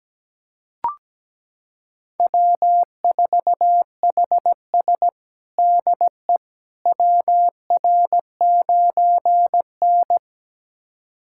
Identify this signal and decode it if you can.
Morse code, heard as a keyed tone, about 17 wpm, 700 Hz. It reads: W4HS DE WR9N
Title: USA Callsign DE Callsign - 4 Characters - 17wpm